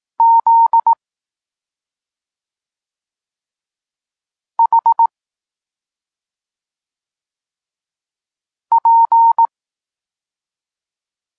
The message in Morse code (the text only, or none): Z H P